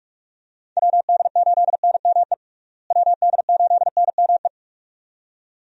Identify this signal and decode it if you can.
Morse code, heard as a keyed tone, about 41 wpm, 700 Hz. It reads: WD8NME WD8NME
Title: USA Call Signs - 6 Characters - Repeated 2x - 45wpm